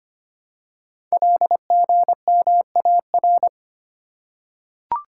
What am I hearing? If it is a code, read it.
Morse code, heard as a keyed tone, about 25 wpm, 700 Hz. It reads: LGMAR
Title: Code Groups - 5 Letters - 25wpm